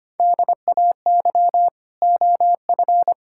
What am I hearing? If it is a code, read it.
Morse code, heard as a keyed tone, about 25 wpm, 700 Hz. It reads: DAY OF
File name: practice / 25wpm / Sentences with 9-Word Phrases - 25wpm